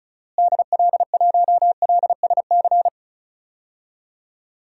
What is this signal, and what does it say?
Morse code, heard as a keyed tone, about 34 wpm, 700 Hz. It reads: DL1LSC
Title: DXCC-100 Call Signs - 6 Characters - Repeated 2x - 35wpm